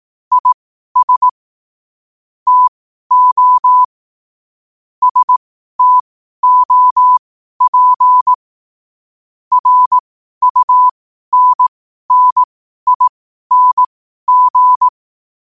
Morse code, heard as a keyed tone, about 18 wpm, spaced as for 12 wpm, 1000 Hz. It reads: IS TO STOP RUNNING
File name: qotdfiles / 12wpm